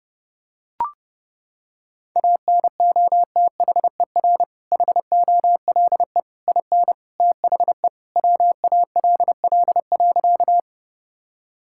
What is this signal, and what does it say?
Morse code, heard as a keyed tone, about 28 wpm, 700 Hz. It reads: ANOTHER HOLE IN THE WALL.
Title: Sentences from Top 1600 Words 30wpm